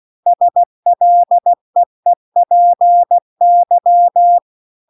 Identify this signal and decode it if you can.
Morse code, heard as a keyed tone, about 16 wpm, 700 Hz. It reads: SLEEPY